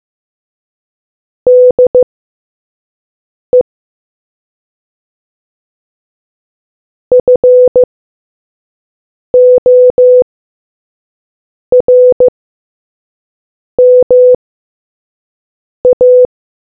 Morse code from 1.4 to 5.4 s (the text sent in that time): DE